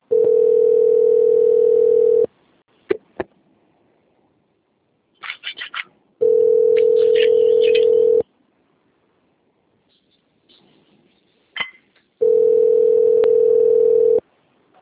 Ringback tone (North America), 440 + 480 Hz.
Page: Sao Carlos II